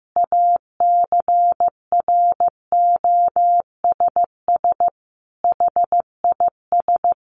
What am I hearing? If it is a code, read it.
Morse code, heard as a keyed tone, about 15 wpm, 700 Hz. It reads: ACROSS HIS
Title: Sentences from Top 700 Words 15wpm